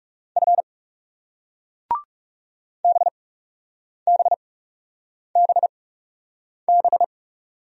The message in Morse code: F BBBB

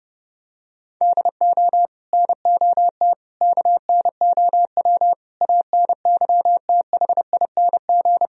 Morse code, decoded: DO NOT KNOW ANYTHING

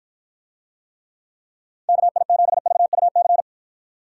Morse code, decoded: KI7VUC

